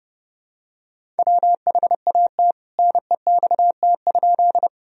Morse code, heard as a keyed tone, about 31 wpm, 700 Hz. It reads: WHAT NEXT?
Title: Sentences from Top 1000 Words - Review Part 1 - 30wpm